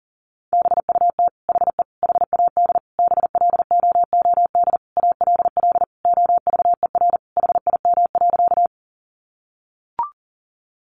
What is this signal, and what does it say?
Morse code, heard as a keyed tone, about 40 wpm, 700 Hz. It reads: BUT HE HAD BLOOD ALL OVER HIM.
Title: Sentences from Top 1100 Words 40wpm